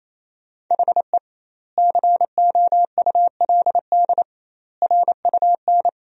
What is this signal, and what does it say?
Morse code, heard as a keyed tone, about 28 wpm, 700 Hz. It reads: HE COULD RUN
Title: 2x Word-Spacing - Sentences from Top 300 Words - 28wpm